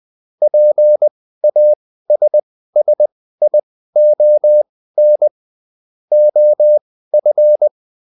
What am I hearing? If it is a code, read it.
Morse code, heard as a keyed tone, about 20 wpm, spaced as for 14 wpm, 600 Hz. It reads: PASSION OF